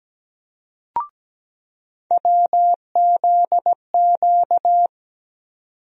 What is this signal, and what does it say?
Morse code, heard as a keyed tone, about 17 wpm, 700 Hz. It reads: WZQ
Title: USA Call Sign Postfixes - 3 Letters 17wpm